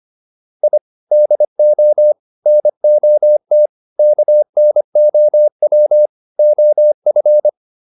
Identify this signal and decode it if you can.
Morse code, heard as a keyed tone, about 25 wpm, 600 Hz. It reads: I DO NOT KNOW OF